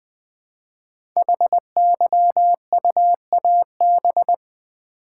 Morse code, heard as a keyed tone, about 20 wpm, 700 Hz. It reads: HYUAB
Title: Code Groups - 5 Letter:Numbers - 20wpm